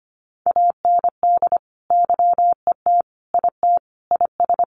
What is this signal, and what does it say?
Morse code, heard as a keyed tone, about 25 wpm, 700 Hz. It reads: AND YET IT IS
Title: Sentences from Top 1400 Words 25wpm